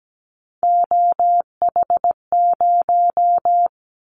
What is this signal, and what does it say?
Morse code, heard as a keyed tone, about 17 wpm, 700 Hz. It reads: OH0
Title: Callsign Prefixes for Top 100 DXCC - 17wpm